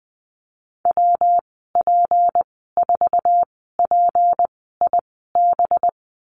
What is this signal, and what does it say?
Morse code, heard as a keyed tone, about 20 wpm, 700 Hz. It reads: W P 4 P I B
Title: 231 Lesson 37g - Letter x - Call Signs 20wpm